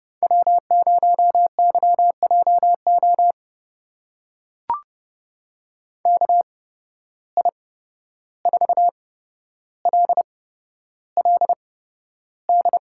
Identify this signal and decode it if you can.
Morse code, heard as a keyed tone, about 31 wpm, spaced as for 12 wpm, 700 Hz. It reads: W0YJO KI4LLD
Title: USA Call Signs - 8x Character Spacing - 30wpm